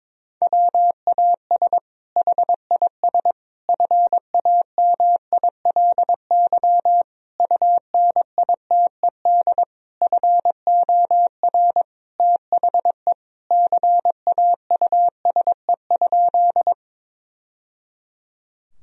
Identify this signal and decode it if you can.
Morse code, heard as a keyed tone, about 22 wpm, 700 Hz. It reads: WAS HIS FAMILY UNITED FOR THE CAUSE?
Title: Sentences from Top 1000 Words - Review Part 4 - 22wpm